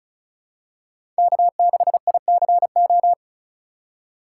Morse code, heard as a keyed tone, about 36 wpm, 700 Hz. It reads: K6ICO
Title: USA Call Signs - 5 Characters - 4x Character Spacing - 35wpm